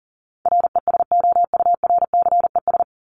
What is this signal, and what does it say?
Morse code, heard as a keyed tone, about 40 wpm, 700 Hz. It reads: RESOURCES
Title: ES:S Suffix - Top 100 Words with Base Words 40wpm